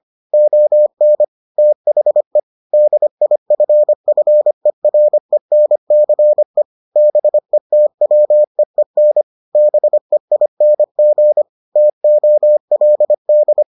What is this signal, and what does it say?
Morse code, heard as a keyed tone, about 25 wpm, 600 Hz. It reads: ON THE DIFFERENCE BETWEEN BEING TOLD